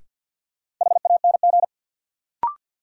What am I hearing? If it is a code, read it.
Morse code, heard as a keyed tone, about 52 wpm, 700 Hz. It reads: HANG
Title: Sentences from Top 1600 Words - New Word Drill Alt - 50wpm